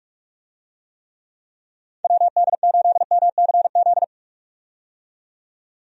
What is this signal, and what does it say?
Morse code, heard as a keyed tone, about 46 wpm, 700 Hz. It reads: WD8MCZ
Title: USA Call Signs - 6 Characters - 45wpm